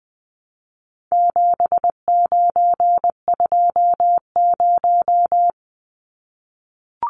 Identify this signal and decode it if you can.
Morse code, heard as a keyed tone, about 20 wpm, 700 Hz. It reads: 7920
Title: Numbers - 4-Digit 20wpm